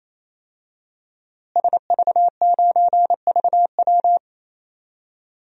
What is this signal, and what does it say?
Morse code, heard as a keyed tone, about 28 wpm, 700 Hz. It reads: SV9VW
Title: DXCC-100 Call Signs - 5 Characters - Repeated 3x - 28wpm